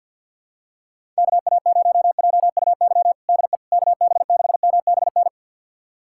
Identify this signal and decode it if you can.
Morse code, heard as a keyed tone, about 51 wpm, 700 Hz. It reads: KA0JUY DE KB6MBN
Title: USA Callsign DE Callsign 50wpm